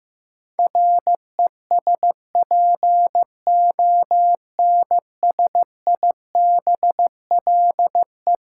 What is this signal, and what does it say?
Morse code, heard as a keyed tone, about 15 wpm, 700 Hz. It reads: RESPONSIBLE
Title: Sentences from Top 1000 Words - Review Part 7 - New Word Drill - 15wpm